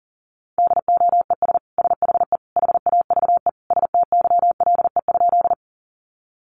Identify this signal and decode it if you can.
Morse code, heard as a keyed tone, about 40 wpm, 700 Hz. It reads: DOES SHE HAVE STYLE?
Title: Sentences from Top 800 Words 40wpm